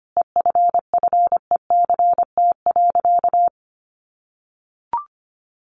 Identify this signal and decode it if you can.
Morse code, heard as a keyed tone, about 25 wpm, 700 Hz. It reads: EFFECT.
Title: Sentences from Top 1400 Words 25wpm